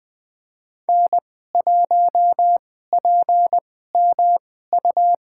Morse code, heard as a keyed tone, about 20 wpm, spaced as for 14 wpm, 700 Hz. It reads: N1PMU
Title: USA Call Signs - 2x Character Spacing - 20wpm